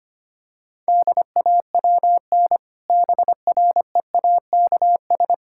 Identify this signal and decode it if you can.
Morse code, heard as a keyed tone, about 25 wpm, 700 Hz. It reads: DAWN BREAKS
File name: practice / 25wpm / Difficult Sentences with Hints - 25wpm